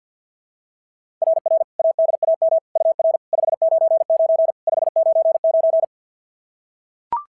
Morse code, heard as a keyed tone, about 45 wpm, 660 Hz. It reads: RR ADAM UR 599 599
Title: SOTA QSO Practice 50wpm